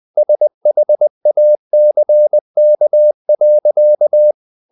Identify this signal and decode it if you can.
Morse code, heard as a keyed tone, about 20 wpm, 600 Hz. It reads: SHACK.
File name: the old man and the sea / the old man and the sea part 1 20wpm